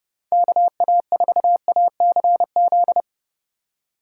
Morse code, heard as a keyed tone, about 36 wpm, 700 Hz. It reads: KA4ACZ
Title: Speed Racing - USA 2x3 Call Signs 30wpm